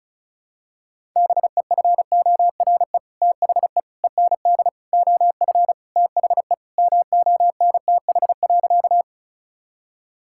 Morse code, heard as a keyed tone, about 36 wpm, 700 Hz. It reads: BEFORE THE END OF THE MONTH.
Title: Sentences from Top 1000 Words - Review Part 5 - 35wpm